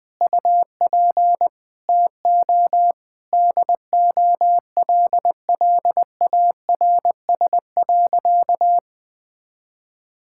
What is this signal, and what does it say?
Morse code, heard as a keyed tone, about 20 wpm, 700 Hz. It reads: UP TO DOLLARS.
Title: Sentences with 9-Word Phrases - 20wpm